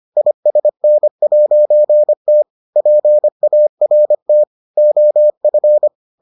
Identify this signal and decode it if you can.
Morse code, heard as a keyed tone, about 25 wpm, 600 Hz. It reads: ISN'T PART OF